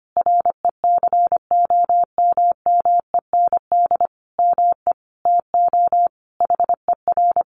Morse code, heard as a keyed tone, about 25 wpm, 700 Hz. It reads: RECOMMEND ME TO HER
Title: Sentences from Top 1200 Words 25wpm